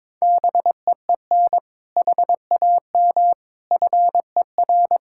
Morse code, heard as a keyed tone, about 21 wpm, 700 Hz. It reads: BEEN HAM FER
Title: Practice QSO - Exchange 3 - More Info - 22wpm